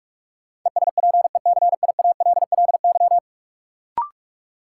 Morse code, heard as a keyed tone, about 45 wpm, 700 Hz. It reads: ESPECIALLY
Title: Sentences from Top 700 Words - New Word Drill - 45wpm